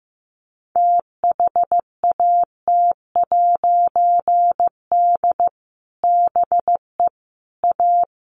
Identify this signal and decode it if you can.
Morse code, heard as a keyed tone, about 15 wpm, 700 Hz. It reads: THAT'D BE A
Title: A Sound Of Thunder 15wpm